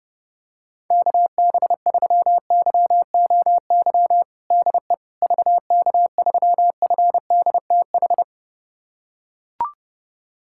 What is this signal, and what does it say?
Morse code, heard as a keyed tone, about 30 wpm, 700 Hz. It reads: KB3YOY DE VK3FDTH
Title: USA Callsign DE Callsign 30wpm